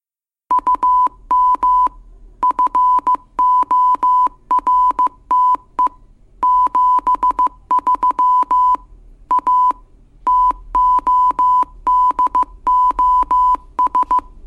Morse code, heard as a keyed tone, about 15 wpm, 1000 Hz. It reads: UM FORTE 73 A TODOS